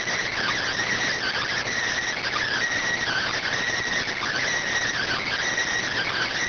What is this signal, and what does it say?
Emergency Alert System (SAME) tones (header bursts), heard as harsh buzzing bursts. It is heard from 1.2 to 5.4 s.